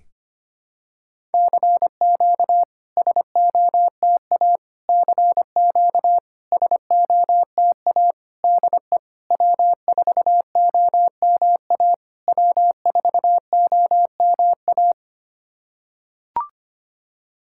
Morse code, heard as a keyed tone, about 25 wpm, 690 Hz. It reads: CQ SOTA CQ SOTA DE W4OMA W4OMA
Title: SOTA QSO Practice 25wpm